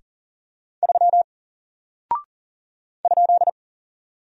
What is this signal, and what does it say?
Morse code, heard as a keyed tone, about 40 wpm, 700 Hz. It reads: 3 ?
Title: Single Character (Letter:Number:Punctuation:Prosign) Alt 40wpm